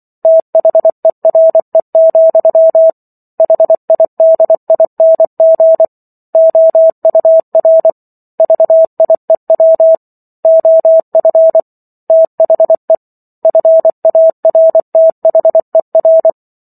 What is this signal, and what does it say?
Morse code, heard as a keyed tone, about 24 wpm, 650 Hz. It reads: THERE, HIDING OUR VIEW OF THE FARTHER
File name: Book One / 15-Chapter 15 - 24 WPM